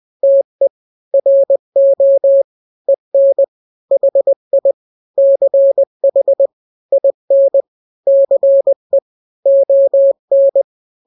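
Morse code, heard as a keyed tone, about 20 wpm, 550 Hz. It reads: TE RO EN HI CH IN CE ON
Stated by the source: Morse drill of two-letter n-grams at 20/18 WPM (Farnsworth), from KK7RR.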